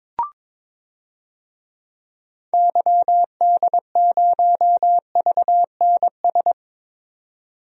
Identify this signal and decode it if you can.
Morse code, heard as a keyed tone, about 22 wpm, 700 Hz. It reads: YD0VNS
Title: DXCC-100 Call Signs - 6 Characters - 22wpm